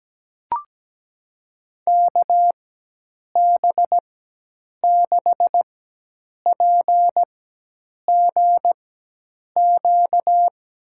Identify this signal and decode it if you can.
Morse code, heard as a keyed tone, about 17 wpm, spaced as for 8 wpm, 700 Hz. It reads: KB6PGQ
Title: USA Call Signs - 6 Characters - 4x Character Spacing - 17wpm